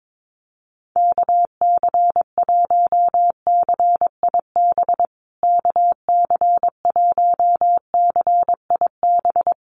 Morse code, heard as a keyed tone, about 22 wpm, 700 Hz. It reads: KC1CIB KC1CIB
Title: USA Call Signs Repeated 3x 22wpm